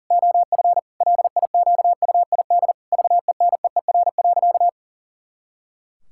Morse code, heard as a keyed tone, about 37 wpm, 700 Hz. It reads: OF LIQUID VENEER.
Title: Pangram Challenge v2 - 40wpm